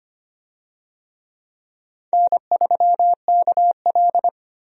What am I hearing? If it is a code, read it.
Morse code, heard as a keyed tone, about 25 wpm, 700 Hz. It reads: N3KL
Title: USA Call Signs - 4 Characters - 25wpm